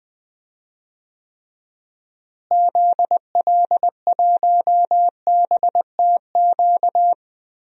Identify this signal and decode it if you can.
Morse code, heard as a keyed tone, about 20 wpm, 700 Hz. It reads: ZL1BTQ
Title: DXCC-100 Call Signs - 6 Characters - Repeated 2x - 20wpm